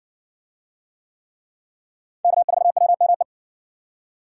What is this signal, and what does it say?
Morse code, heard as a keyed tone, about 57 wpm, 700 Hz. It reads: K4WME